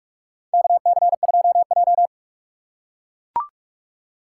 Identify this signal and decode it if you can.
Morse code, heard as a keyed tone, about 41 wpm, 700 Hz. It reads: KC2J
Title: USA Call Signs - 4 Characters - 6x Character Spacing - 45wpm